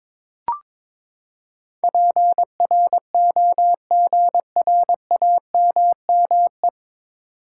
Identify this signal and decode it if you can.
Morse code, heard as a keyed tone, about 22 wpm, 700 Hz. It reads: PROGRAMME
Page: Morse Code Ninja practice 22wpm